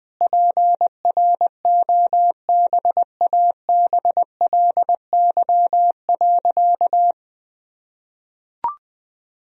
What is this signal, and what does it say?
Morse code, heard as a keyed tone, about 20 wpm, 700 Hz. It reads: PROBABLY.
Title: Sentences from Top 1000 Words - Review Part 4 - 20wpm